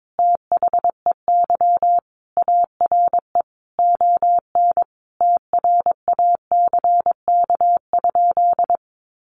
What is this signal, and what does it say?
Morse code, heard as a keyed tone, about 22 wpm, 700 Hz. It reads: THEY ARE ON TRACK?
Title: Sentences from Top 1200 Words 22wpm